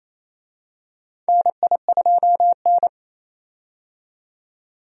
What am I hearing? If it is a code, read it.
Morse code, heard as a keyed tone, about 27 wpm, 700 Hz. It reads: NI2N
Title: USA Call Signs - 4 Characters - 28wpm